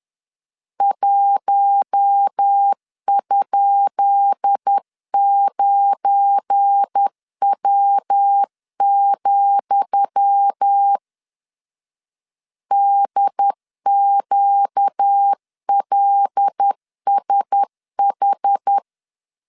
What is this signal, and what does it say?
Morse code, heard as a keyed tone, about 10 wpm, 800 Hz. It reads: 1?9W, DQLSH
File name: LEZIONI / 10 SIGNS / 10D COMA 10wpm 3013